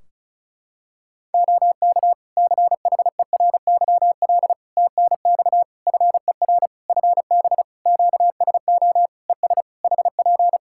Morse code, heard as a keyed tone, about 34 wpm, 700 Hz. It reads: OK CHERYL TNX FER FB QSO ES HP
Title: Practice QSO - Exchange 4 - Ending - 35wpm